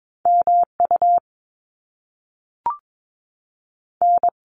Morse code, heard as a keyed tone, about 22 wpm, 700 Hz. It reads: MU N
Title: Top Two Letter Combinations in Words 22wpm